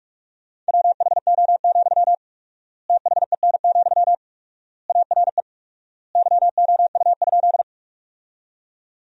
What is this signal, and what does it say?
Morse code, heard as a keyed tone, about 44 wpm, 700 Hz. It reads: WHO, THEN, ARE YOU?
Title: 4x Word-Spacing - Sentences from Top 100 Words - 45wpm